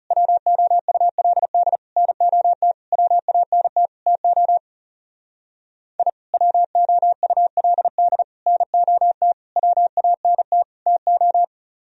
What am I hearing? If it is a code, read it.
Morse code, heard as a keyed tone, about 36 wpm, 700 Hz. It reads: WOULD NOT WANT TO I WOULD NOT WANT TO
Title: Speed-Racing - Sets of 5 Words 30wpm